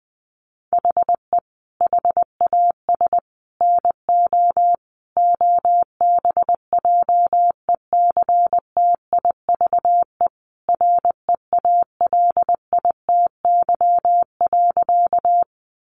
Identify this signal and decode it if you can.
Morse code, heard as a keyed tone, about 20 wpm, 700 Hz. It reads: HE HAS NO OBJECTIVE REALITY.
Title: Sentences from Top 1200 Words 20wpm